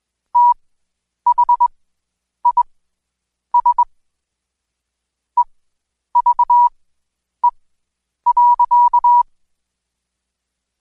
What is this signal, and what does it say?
Morse code, heard as a keyed tone, about 21 wpm, spaced as for 9 wpm, 970 Hz. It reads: THIS EVE.
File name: Code Quick / AudioDownld2 / Seg115-Day20-DayDrill 20